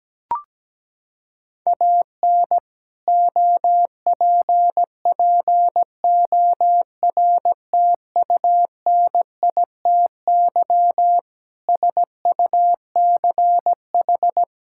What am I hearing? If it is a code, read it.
Morse code, heard as a keyed tone, about 17 wpm, 700 Hz. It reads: AN OPPORTUNITY SUCH
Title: Sentences from Top 1600 Words 17wpm